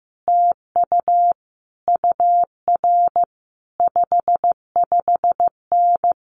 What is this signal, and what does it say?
Morse code, heard as a keyed tone, about 15 wpm, 700 Hz. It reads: TU UR 55N
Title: Common QSO Phrases Encore 15wpm